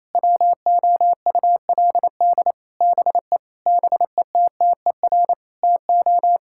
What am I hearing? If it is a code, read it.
Morse code, heard as a keyed tone, about 28 wpm, 700 Hz. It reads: WOULD BE BETTER TO
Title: Sentences from Top 1600 Words 28wpm